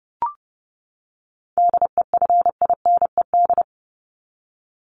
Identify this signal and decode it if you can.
Morse code, heard as a keyed tone, about 30 wpm, 700 Hz. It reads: DEFINED
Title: ED Suffix - Top 100 Words with Base Words 30wpm